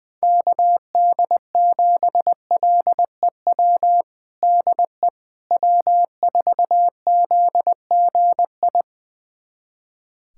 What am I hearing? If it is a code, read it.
Morse code, heard as a keyed tone, about 21 wpm, 690 Hz. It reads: KD7LEW DE W4ZGI
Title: USA Callsign DE Callsign 20wpm